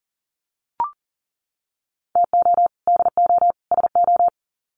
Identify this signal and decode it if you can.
Morse code, heard as a keyed tone, about 40 wpm, 700 Hz. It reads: TO DO SO
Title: Sets of 3 Words 40wpm